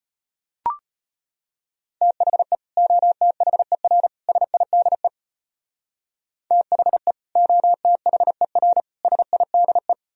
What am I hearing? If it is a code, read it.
Morse code, heard as a keyed tone, about 34 wpm, 700 Hz. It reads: THE OTHER SIDE THE OTHER SIDE